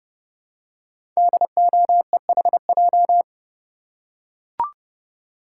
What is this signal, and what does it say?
Morse code, heard as a keyed tone, about 30 wpm, 700 Hz. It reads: DOEHJ
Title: Code Groups - 5 Letters - 30wpm